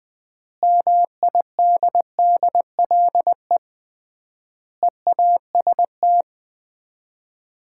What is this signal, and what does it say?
Morse code, heard as a keyed tone, about 20 wpm, 700 Hz. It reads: MIDDLE EAST
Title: 3x Word-Spacing - Sets of 3 Words - 20wpm